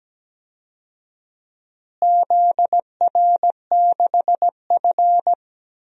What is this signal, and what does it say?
Morse code, heard as a keyed tone, about 17 wpm, 700 Hz. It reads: ZR6F
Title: DXCC-100 Call Signs - 4 Characters - Repeated 3x - 17wpm